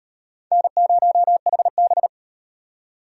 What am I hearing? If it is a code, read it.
Morse code, heard as a keyed tone, about 36 wpm, 700 Hz. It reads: N0HB